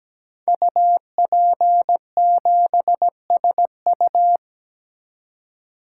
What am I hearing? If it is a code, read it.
Morse code, heard as a keyed tone, about 17 wpm, 700 Hz. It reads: UP7SU
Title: Code Groups - 5 Letter:Numbers - 17wpm